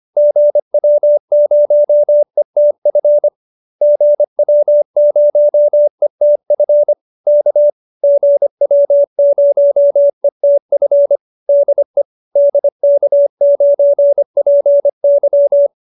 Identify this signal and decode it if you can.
Morse code, heard as a keyed tone, about 25 wpm, 590 Hz. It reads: GW0ETF GW0ETF K GW0ETF DE DK9PY
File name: QSO318 25